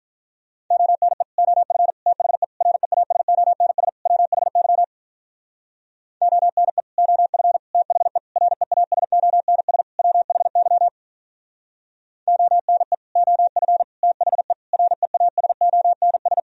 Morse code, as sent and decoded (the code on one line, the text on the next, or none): --- -. / --- ..-. / - .... / .-. .- ... --- -. ... / .-- .... -.-- / --- -. / --- ..-. / - .... / .-. .- ... --- -. ... / .-- .... -.-- / --- -. / --- ..-. / - .... / .-. .- ... --- -. ...
ONE OF THE REASONS WHY ONE OF THE REASONS WHY ONE OF THE REASONS